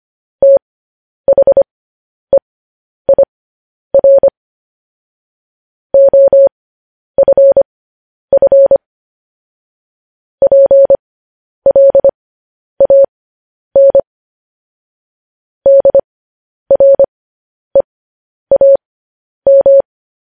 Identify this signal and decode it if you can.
Morse code, heard as a keyed tone, about 25 wpm, spaced as for 10 wpm, 550 Hz. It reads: THEIR OFF PLAN DREAM
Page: news headlines in morse code at 10 wpm